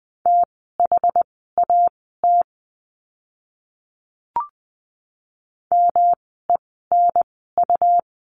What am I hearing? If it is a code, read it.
Morse code, heard as a keyed tone, about 20 wpm, spaced as for 14 wpm, 700 Hz. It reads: THAT MENU
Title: 108 Lesson 18f - Letter m - Words 20wpm